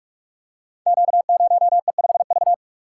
Morse code, heard as a keyed tone, about 44 wpm, 700 Hz. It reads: Q0E5V
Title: Code Groups - 5 Letter:Numbers - 45wpm